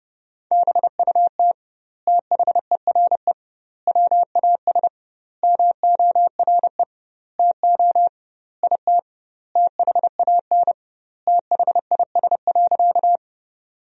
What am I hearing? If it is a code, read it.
Morse code, heard as a keyed tone, about 30 wpm, 700 Hz. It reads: BUT THERE WAS MORE TO IT THAN THIS.
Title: 2x Word-Spacing - Sentences from Top 100 Words - 30wpm